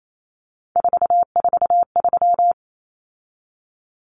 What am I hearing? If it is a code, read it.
Morse code, heard as a keyed tone, about 28 wpm, 700 Hz. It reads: 443
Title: Numbers - 3-Digit 28wpm